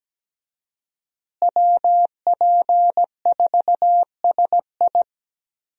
Morse code, heard as a keyed tone, about 17 wpm, 700 Hz. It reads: WP4SI
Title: USA Call Signs - 5 Characters - Repeated 3x - 17wpm